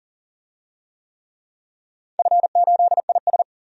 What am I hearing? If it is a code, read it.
Morse code, heard as a keyed tone, about 42 wpm, 700 Hz. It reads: F8IS
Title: DXCC-100 Call Signs - 4 Characters - 40wpm